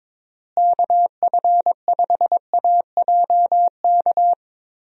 Morse code, heard as a keyed tone, about 22 wpm, 700 Hz. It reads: KF5AJK
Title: USA Call Signs - 6 Characters - Repeated 2x - 22wpm